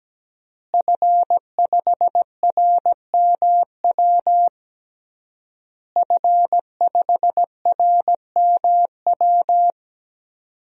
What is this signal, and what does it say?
Morse code, heard as a keyed tone, about 17 wpm, 700 Hz. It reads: F5RMW F5RMW
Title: DXCC-100 Call Signs - 5 Characters - Repeated 3x - 17wpm